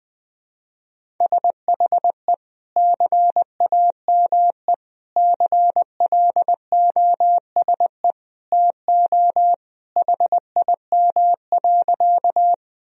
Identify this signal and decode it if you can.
Morse code, heard as a keyed tone, about 21 wpm, 700 Hz. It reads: SHE CAME CLOSE TO HIM.
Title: Sentences from Top 1000 Words - Review Part 1 - 20wpm